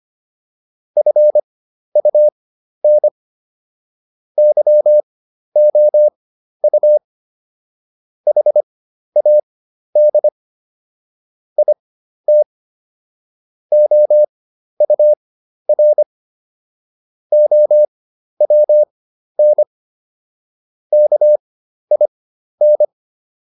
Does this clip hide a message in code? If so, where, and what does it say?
Yes: Morse code from 1.0 to 22.8 s, heard as a keyed tone, about 25 wpm, spaced as for 12 wpm, 600 Hz. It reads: FUN YOU HAD IT OUR OWN KIN